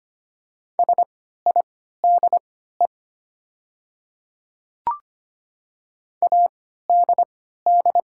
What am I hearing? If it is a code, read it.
Morse code, heard as a keyed tone, about 25 wpm, spaced as for 14 wpm, 700 Hz. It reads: SIDE ADD
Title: 3x Character Spacing - Top 200 Words - 25wpm